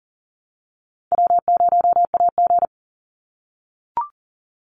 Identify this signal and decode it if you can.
Morse code, heard as a keyed tone, about 40 wpm, 700 Hz. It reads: W0AG
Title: Call Signs - 40wpm